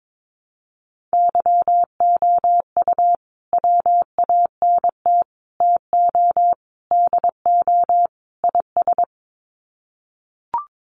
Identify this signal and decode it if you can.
Morse code, heard as a keyed tone, about 22 wpm, 700 Hz. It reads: YOU WANT TO DO IS
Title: Sets of 5 Words Encore2 22wpm